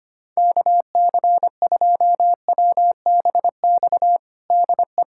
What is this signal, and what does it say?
Morse code, heard as a keyed tone, about 25 wpm, 700 Hz. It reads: KC2WBX DE
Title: USA Callsign DE Callsign - 6 Characters - 25wpm